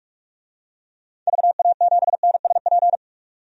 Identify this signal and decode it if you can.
Morse code, heard as a keyed tone, about 46 wpm, 700 Hz. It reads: VA7NSP
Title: DXCC-100 Call Signs - 6 Characters - 45wpm